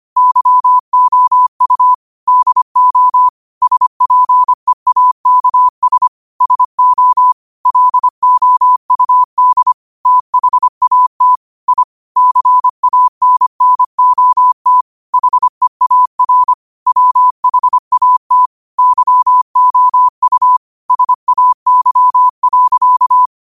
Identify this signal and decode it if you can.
Morse code, heard as a keyed tone, about 25 wpm, 1000 Hz. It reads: YOU DO SPEAKS SO LOUD THAT I CANNOT HEAR WHAT YOU SAY.
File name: qotdfiles / 25wpm